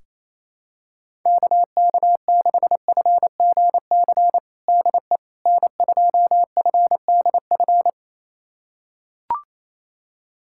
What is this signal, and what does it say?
Morse code, heard as a keyed tone, about 28 wpm, 700 Hz. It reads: KK6FGC DE N2FDF
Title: USA Callsign DE Callsign 28wpm